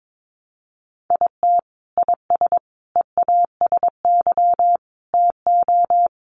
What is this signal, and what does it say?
Morse code, heard as a keyed tone, about 22 wpm, 700 Hz. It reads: IT IS EASY TO
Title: Sentences from Top 1300 Words 22wpm